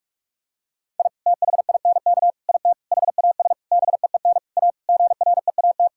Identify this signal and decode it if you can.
Morse code, heard as a keyed tone, about 46 wpm, 700 Hz. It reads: I THINK IT HAS BEEN A GREAT
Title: Sentences from Top 1000 Words - Review Part 6 - 45wpm